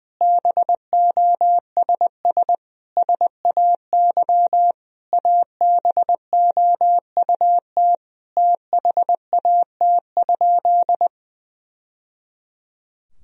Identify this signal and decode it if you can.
Morse code, heard as a keyed tone, about 20 wpm, 700 Hz. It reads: BOSS SAY ABOUT THAT?
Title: Sentences from Top 1000 Words - Review Part 4 - 20wpm